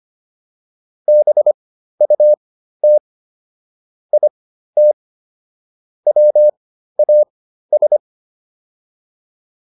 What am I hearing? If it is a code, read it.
Morse code, heard as a keyed tone, about 25 wpm, spaced as for 13 wpm, 600 Hz. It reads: BUT IT WAS